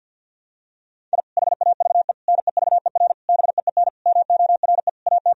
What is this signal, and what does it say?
Morse code, heard as a keyed tone, about 52 wpm, 700 Hz. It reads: I HAVE NEVER BEEN MORE AT
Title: Sentences from Top 1000 Words - Review Part 1 - 50wpm